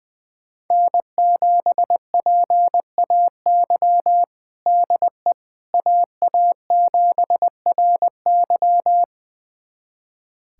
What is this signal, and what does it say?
Morse code, heard as a keyed tone, about 21 wpm, 700 Hz. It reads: N7PAY DE AA7RY
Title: USA Callsign DE Callsign - 5 Characters - 20wpm